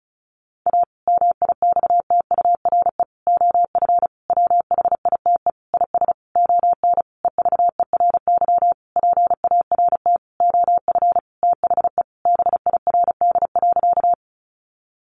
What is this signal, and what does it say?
Morse code, heard as a keyed tone, about 34 wpm, 700 Hz. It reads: A MIXTURE OF WHITE IS ON EVERY PART OF THE BIRD.
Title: Sentences from Top 1500 Words 35wpm